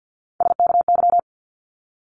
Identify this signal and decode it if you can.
Morse code, heard as a keyed tone, about 50 wpm, 700 Hz. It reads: SKY
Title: Multi-letter Phonemes 50wpm